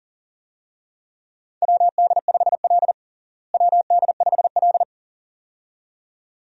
Morse code, heard as a keyed tone, about 40 wpm, 700 Hz. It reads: WD5L WD5L
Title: USA Call Signs - 4 Characters - Repeated 2x - 40wpm